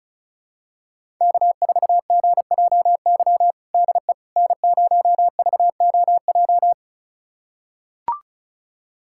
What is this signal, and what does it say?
Morse code, heard as a keyed tone, about 36 wpm, 700 Hz. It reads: K4GJY DE N0VOJ
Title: USA Callsign DE Callsign - 5 Characters - 35wpm